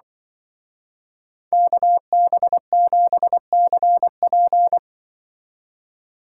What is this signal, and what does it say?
Morse code, heard as a keyed tone, about 23 wpm, 700 Hz. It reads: KB7CP
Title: Speed Racing - USA 2x2 Call Signs 20wpm